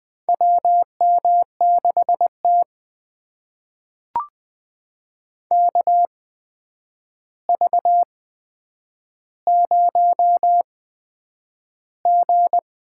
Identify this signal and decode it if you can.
Morse code, heard as a keyed tone, about 20 wpm, spaced as for 9 wpm, 700 Hz. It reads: WM6T KV0G
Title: USA Call Signs - 4 Characters - 8x Character Spacing - 20wpm